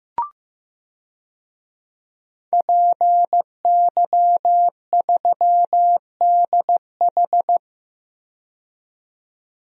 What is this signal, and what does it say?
Morse code, heard as a keyed tone, about 15 wpm, 700 Hz. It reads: PY3DH
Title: DXCC-100 Call Signs - 5 Characters - 15wpm